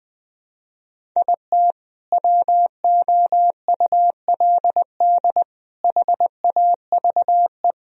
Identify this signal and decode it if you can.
Morse code, heard as a keyed tone, about 19 wpm, 700 Hz. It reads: IT WOULD HAVE
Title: Sentences from Top 1000 Words - Review Part 4 - 20wpm